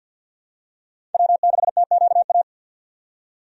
Morse code, heard as a keyed tone, about 48 wpm, 700 Hz. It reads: W6TQA
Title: USA Call Signs - 5 Characters - 50wpm